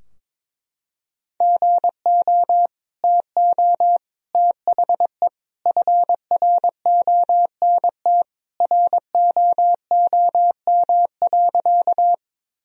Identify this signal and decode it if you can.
Morse code, heard as a keyed tone, about 21 wpm, 700 Hz. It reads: GO TO THE FRONT ROOM.
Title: Sentences from Top 1000 Words - Review Part 2 - 22wpm